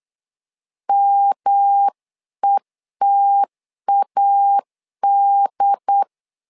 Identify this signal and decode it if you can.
Morse code, heard as a keyed tone, about 8 wpm, 800 Hz. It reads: METAD
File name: LEZIONI / 3 MWRO / 3 MW 8wpm 2501